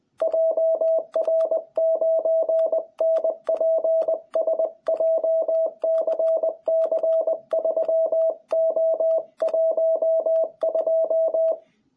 Morse code, heard as a keyed tone, about 20 wpm, 660 Hz. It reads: JR9NPS JCC3O12
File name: jh8cbh / 2021cw